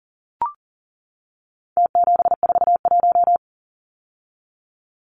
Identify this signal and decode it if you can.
Morse code, heard as a keyed tone, about 40 wpm, 700 Hz. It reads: T741
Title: Numbers - Contest Style - 4-Digit 40wpm